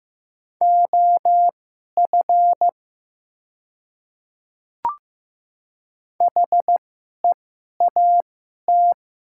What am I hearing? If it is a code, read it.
Morse code, heard as a keyed tone, about 15 wpm, spaced as for 11 wpm, 690 Hz. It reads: OF HEAT
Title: 2x Character Spacing - Top 500 Words - 15wpm